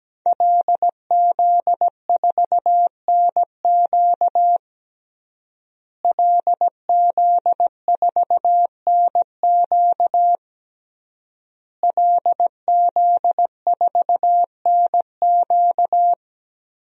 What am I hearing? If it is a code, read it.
Morse code, heard as a keyed tone, about 17 wpm, 700 Hz. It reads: LZ4NQ LZ4NQ LZ4NQ
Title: DXCC-100 Call Signs - 5 Characters - Repeated 3x - 17wpm